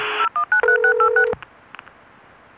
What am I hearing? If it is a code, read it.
DTMF, heard as telephone touch tones, tones of about 100 ms. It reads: 0193686
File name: Modem Music